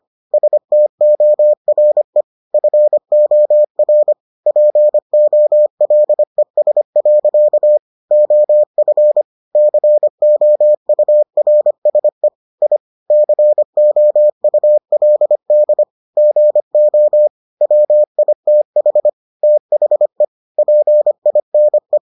Morse code, heard as a keyed tone, about 25 wpm, 600 Hz. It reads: STORE FOR POLES. OF COURSE I COULD GO WITH THE PINE